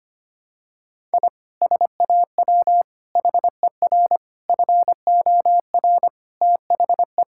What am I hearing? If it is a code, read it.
Morse code, heard as a keyed tone, about 25 wpm, 700 Hz. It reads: I SAW HER FOR THE
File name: practice / 25wpm / Sentences with 4-Word Phrases - 25wpm